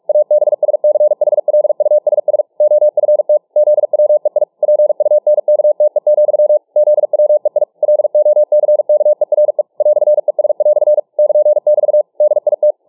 Morse code, heard as a keyed tone, about 46 wpm, 600 Hz. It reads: ABSCHLUSS OFT ZWEI PUNKTE, ZWEI LOCKERE "ES" <KN><BT> DIT